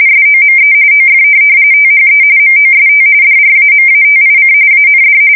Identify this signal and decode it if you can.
RTTY, about 45.45 baud. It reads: SOME OF THE MOST DRAMATIC AND
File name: fsk170 / USS-New-Jersey